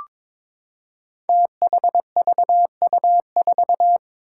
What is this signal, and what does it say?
Morse code, heard as a keyed tone, about 22 wpm, 700 Hz. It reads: THVU4
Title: Five Challenging Letter-Number Combinations 22wpm